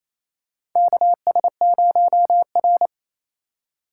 Morse code, heard as a keyed tone, about 28 wpm, 700 Hz. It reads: KS0R
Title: USA Call Signs - 4 Characters - 3x Character Spacing - 28wpm